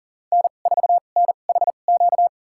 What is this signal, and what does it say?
Morse code, heard as a keyed tone, about 37 wpm, spaced as for 28 wpm, 700 Hz. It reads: N4NHQ
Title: USA Call Signs - 2x Character Spacing - 40wpm